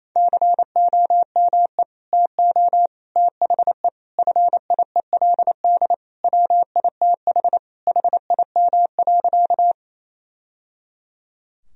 Morse code, heard as a keyed tone, about 28 wpm, 700 Hz. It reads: COME TO THE FIELD WITH HIM.
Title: Sentences from Top 1000 Words - Review Part 2 - 28wpm